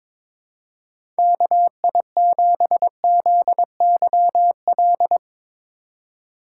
Morse code, heard as a keyed tone, about 22 wpm, 700 Hz. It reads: KI7ZYL
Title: USA Call Signs - 3x Character Spacing - 22wpm